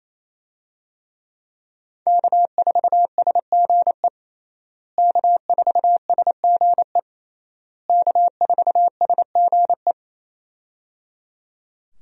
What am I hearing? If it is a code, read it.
Morse code, heard as a keyed tone, about 28 wpm, 700 Hz. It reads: K4SGE K4SGE K4SGE
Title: USA Call Signs - 5 Characters - Repeated 3x - 28wpm